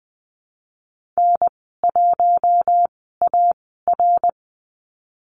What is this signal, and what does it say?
Morse code, heard as a keyed tone, about 20 wpm, spaced as for 15 wpm, 700 Hz. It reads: N1AR
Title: 147 Lesson 24g - Letter y - Call Signs 20wpm